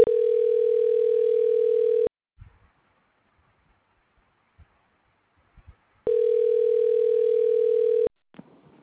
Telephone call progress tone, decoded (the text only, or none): ringback tone